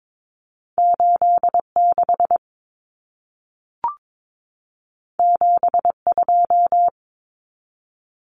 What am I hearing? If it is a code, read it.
Morse code, heard as a keyed tone, about 22 wpm, 700 Hz. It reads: 86 72
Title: Common Cut Numbers - 2-Digit 22wpm